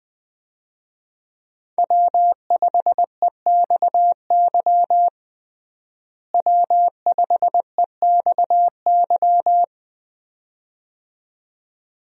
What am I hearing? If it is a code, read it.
Morse code, heard as a keyed tone, about 21 wpm, 700 Hz. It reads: W5EXY W5EXY
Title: USA Call Signs - 5 Characters - Repeated 2x - 20wpm